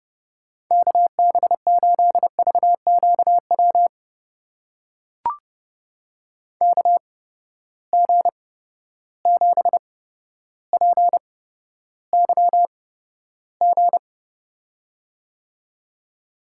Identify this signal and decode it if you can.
Morse code, heard as a keyed tone, about 30 wpm, spaced as for 13 wpm, 700 Hz. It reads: KB8VQW KG7PYG
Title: USA Call Signs - 6 Characters - 8x Character Spacing - 30wpm